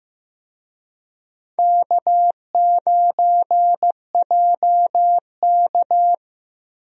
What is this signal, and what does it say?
Morse code, heard as a keyed tone, about 16 wpm, 700 Hz. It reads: K9JK
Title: USA Call Signs - 4 Characters - Repeated 2x - 15wpm